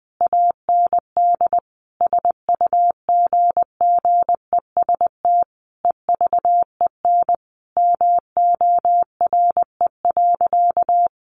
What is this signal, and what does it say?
Morse code, heard as a keyed tone, about 20 wpm, 700 Hz. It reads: AND SUGGEST EVEN MORE.
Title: Sentences from Top 1300 Words 20wpm